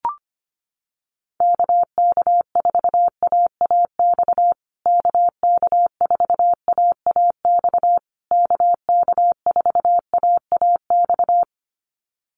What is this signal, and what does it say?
Morse code, heard as a keyed tone, about 25 wpm, 700 Hz. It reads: KK4AAX KK4AAX KK4AAX
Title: USA Call Signs Repeated 3x 25wpm